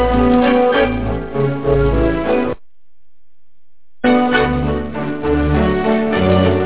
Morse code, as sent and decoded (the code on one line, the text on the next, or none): none